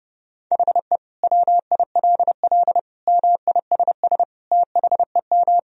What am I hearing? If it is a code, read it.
Morse code, heard as a keyed tone, about 32 wpm, 700 Hz. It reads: HE WILL MISS THEM
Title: Sentences from Top 1000 Words - Review Part 3 - 30wpm